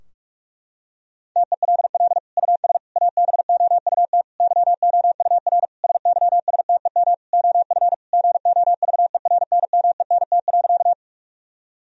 Morse code, heard as a keyed tone, about 46 wpm, 700 Hz. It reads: TELL US ABOUT YOUR SYSTEM OF GOVERNMENT.
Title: Sentences from Top 1000 Words - Review Part 3 - 45wpm